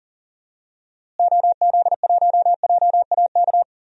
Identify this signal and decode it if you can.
Morse code, heard as a keyed tone, about 39 wpm, 700 Hz. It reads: OZ1JAK
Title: DXCC-100 Call Signs - 6 Characters - Repeated 2x - 40wpm